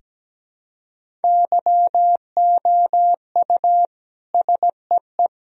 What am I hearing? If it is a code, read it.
Morse code, heard as a keyed tone, about 17 wpm, 700 Hz. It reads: YOU SEE
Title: Sentences from Top 1000 Words - Review Part 1 - 17wpm